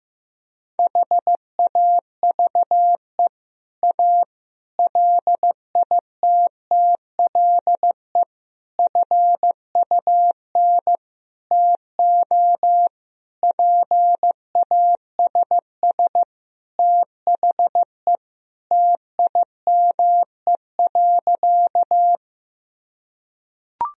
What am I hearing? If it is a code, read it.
Morse code, heard as a keyed tone, about 15 wpm, 700 Hz. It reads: HAVE A LITTLE FUN TO PASS THE TIME.
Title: Sentences from Top 1000 Words - Review Part 4 - 15wpm